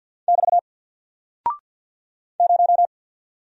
Morse code, heard as a keyed tone, about 50 wpm, 700 Hz. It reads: <BT> 0
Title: Single Character (Letter:Number:Punctuation:Prosign) 50wpm